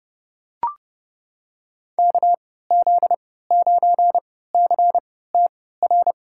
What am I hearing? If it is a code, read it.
Morse code, heard as a keyed tone, about 31 wpm, spaced as for 17 wpm, 700 Hz. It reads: KZ9CTR
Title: USA Call Signs - 3x Character Spacing - 30wpm